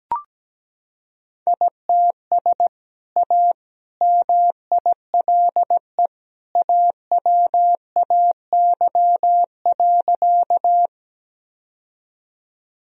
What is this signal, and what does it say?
Morse code, heard as a keyed tone, about 17 wpm, 700 Hz. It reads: ITS A MILE AWAY.
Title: Sentences from Top 1000 Words - Review Part 1 - 17wpm